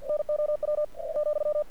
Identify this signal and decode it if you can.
Morse code, heard as a keyed tone, about 48 wpm, 600 Hz. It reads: NOW ,
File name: archived / CRSnet